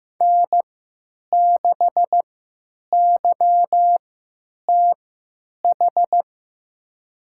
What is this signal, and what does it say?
Morse code, heard as a keyed tone, about 15 wpm, spaced as for 9 wpm, 700 Hz. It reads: N6YTH
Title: USA Call Signs - 5 Characters - 3x Character Spacing - 15wpm